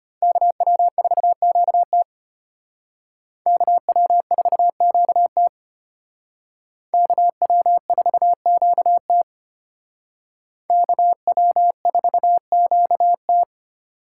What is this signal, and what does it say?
Morse code, heard as a keyed tone, about 33 wpm, 700 Hz. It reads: KW4QT KW4QT KW4QT KW4QT